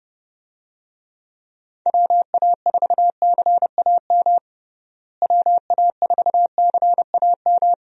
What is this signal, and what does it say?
Morse code, heard as a keyed tone, about 30 wpm, 700 Hz. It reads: WA4CAM WA4CAM
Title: USA Call Signs - 6 Characters - Repeated 2x - 30wpm